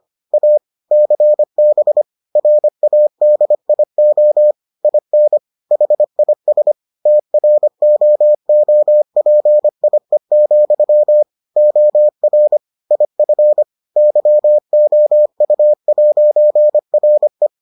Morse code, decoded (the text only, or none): A CB RADIO IN HIS TROOPIE, OR IF YOU'RE